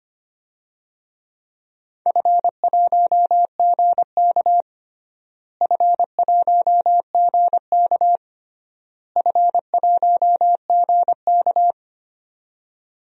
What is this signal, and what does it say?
Morse code, heard as a keyed tone, about 25 wpm, 700 Hz. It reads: F1GK F1GK F1GK
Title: DXCC-100 Call Signs - 4 Characters - Repeated 3x - 25wpm